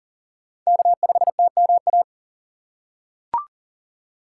Morse code, heard as a keyed tone, about 40 wpm, 700 Hz. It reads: K5TMA
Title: USA Call Signs - 5 Characters - 2x Character Spacing - 40wpm